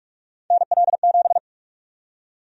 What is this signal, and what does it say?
Morse code, heard as a keyed tone, about 43 wpm, 700 Hz. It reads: NL7
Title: USA Call Sign Prefixes - Repeated 2x - 45wpm